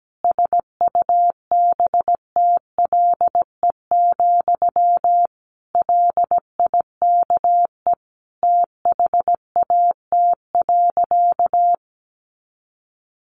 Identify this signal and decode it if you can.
Morse code, heard as a keyed tone, about 17 wpm, 700 Hz. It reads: SUBTLE, LIKE THAT.
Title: A Sound Of Thunder 17wpm